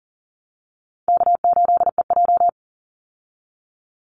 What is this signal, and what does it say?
Morse code, heard as a keyed tone, about 40 wpm, 700 Hz. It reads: K8EJ
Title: Call Signs Repeated 3x - 40wpm